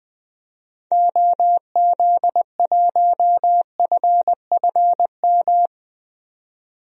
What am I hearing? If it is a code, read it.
Morse code, heard as a keyed tone, about 20 wpm, 700 Hz. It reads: OZ1FFM
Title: DXCC-100 Call Signs - 6 Characters - 20wpm